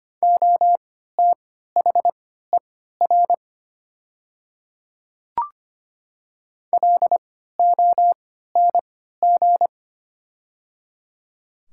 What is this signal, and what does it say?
Morse code, heard as a keyed tone, about 25 wpm, spaced as for 14 wpm, 700 Hz. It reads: OTHER LONG
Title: 3x Character Spacing - Top 100 Words - 25wpm